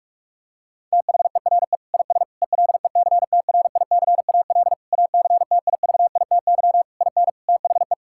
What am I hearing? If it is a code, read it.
Morse code, heard as a keyed tone, about 46 wpm, 690 Hz. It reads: THERE IS ELECTRICAL ACTIVITY IN THE
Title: Sentences from Top 1000 Words - Review Part 6 - 45wpm